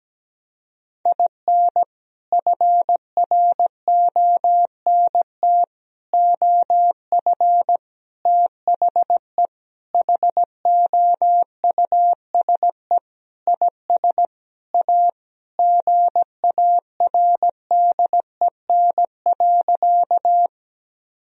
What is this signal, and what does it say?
Morse code, heard as a keyed tone, about 17 wpm, 700 Hz. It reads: IN FRONT OF THE HOUSE IS A GARDEN.
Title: Sentences from Top 1000 Words - Review Part 4 - 17wpm